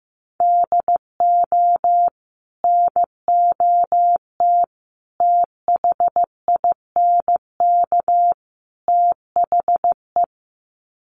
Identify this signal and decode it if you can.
Morse code, heard as a keyed tone, about 14 wpm, 700 Hz. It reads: DO NOT THINK THE
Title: Sets of 4 Words 15wpm